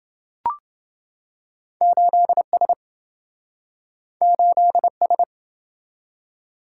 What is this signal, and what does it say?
Morse code, heard as a keyed tone, about 27 wpm, 700 Hz. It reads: 8S 8S